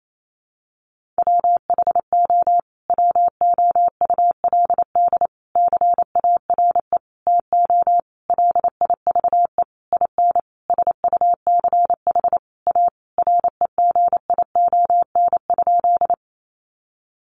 Morse code, heard as a keyed tone, about 28 wpm, 700 Hz. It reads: WHO WOULD CARE TO LIVE IN SUCH A REGION?
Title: Sentences from Top 1500 Words 28wpm